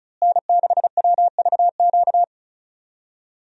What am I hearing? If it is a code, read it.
Morse code, heard as a keyed tone, about 36 wpm, 700 Hz. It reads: N6WVQ
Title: USA Call Signs - 5 Characters - Repeated 3x - 35wpm